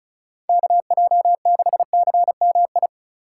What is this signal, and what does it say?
Morse code, heard as a keyed tone, about 43 wpm, 700 Hz. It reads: KJ6CMI
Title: Speed Racing - USA Call Signs 35wpm